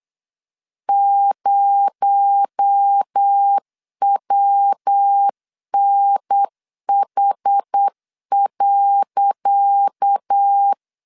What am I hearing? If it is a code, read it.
Morse code, heard as a keyed tone, about 8 wpm, 800 Hz. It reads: 0WNH.
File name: LEZIONI / 10 SIGNS / 10C DOT 8wpm 3649